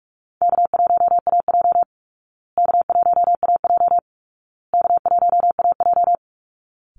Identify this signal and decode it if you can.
Morse code, heard as a keyed tone, about 44 wpm, 700 Hz. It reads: K1AJ K1AJ K1AJ
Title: Call Signs Repeated 2x - 45wpm